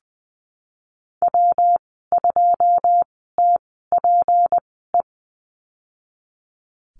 Morse code, heard as a keyed tone, about 20 wpm, spaced as for 15 wpm, 700 Hz. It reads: W2TPE
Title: 245 Lesson 39g - Letter z - Call Signs 20wpm